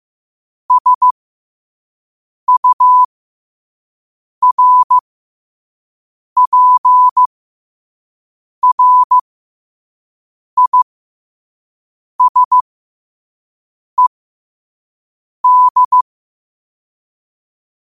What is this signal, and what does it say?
Morse code, heard as a keyed tone, about 15 wpm, spaced as for 5 wpm, 1000 Hz. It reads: SURPRISED
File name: qotdfiles / 5wpm